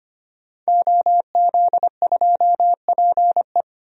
Morse code, heard as a keyed tone, about 25 wpm, 700 Hz. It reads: OZ2PE
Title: DXCC-100 Call Signs - 5 Characters - 25wpm